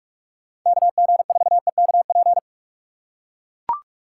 Morse code, heard as a keyed tone, about 43 wpm, 700 Hz. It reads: KG4EKP